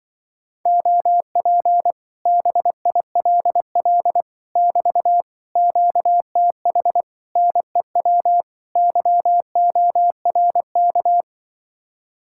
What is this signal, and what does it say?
Morse code, heard as a keyed tone, about 24 wpm, 700 Hz. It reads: OP BILL <BT> QTH NEW YORK
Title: Speed-Racing Common QSO Phrases - 20wpm